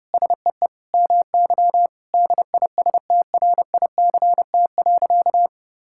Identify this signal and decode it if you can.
Morse code, heard as a keyed tone, about 30 wpm, 700 Hz. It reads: SEE MY DISTRICT.
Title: Sentences from Top 1600 Words 30wpm